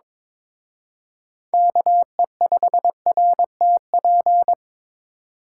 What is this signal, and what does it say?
Morse code, heard as a keyed tone, about 26 wpm, 700 Hz. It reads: KE5RTP
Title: Speed Racing - USA 2x3 Call Signs 22wpm